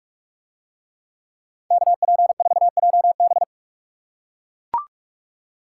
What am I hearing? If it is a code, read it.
Morse code, heard as a keyed tone, about 40 wpm, 700 Hz. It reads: KP4JB